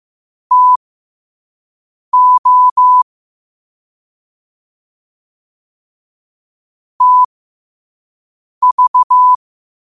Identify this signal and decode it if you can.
Morse code, heard as a keyed tone, about 15 wpm, spaced as for 5 wpm, 1000 Hz. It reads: TO TV
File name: qotdfiles / 5wpm